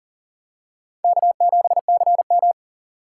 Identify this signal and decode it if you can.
Morse code, heard as a keyed tone, about 40 wpm, 700 Hz. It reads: K7CM
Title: USA Call Signs - 4 Characters - 2x Character Spacing - 40wpm